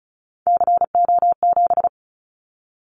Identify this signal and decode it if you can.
Morse code, heard as a keyed tone, about 34 wpm, 700 Hz. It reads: CO7
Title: Callsign Prefixes for Top 100 DXCC - 35wpm